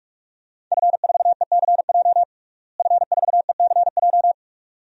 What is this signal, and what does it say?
Morse code, heard as a keyed tone, about 41 wpm, 700 Hz. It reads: F4ECJ F4ECJ
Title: DXCC-100 Call Signs - 5 Characters - Repeated 2x - 45wpm